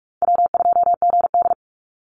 Morse code, heard as a keyed tone, about 44 wpm, 700 Hz. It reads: W2GD
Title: Call Signs - 45wpm